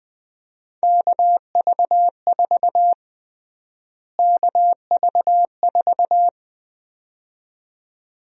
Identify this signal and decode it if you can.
Morse code, heard as a keyed tone, about 20 wpm, 700 Hz. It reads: KV4 KV4
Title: USA Call Sign Prefixes - Repeated 2x - 20wpm